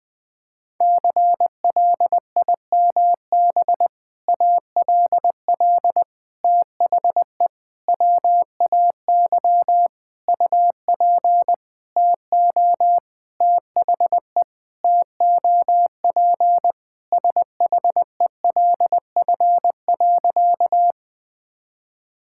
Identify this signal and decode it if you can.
Morse code, heard as a keyed tone, about 21 wpm, 700 Hz. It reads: CLIMB ALL THE WAY UP TO THE TOP SHELF.
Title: Sentences with 6-Word Phrases - 20wpm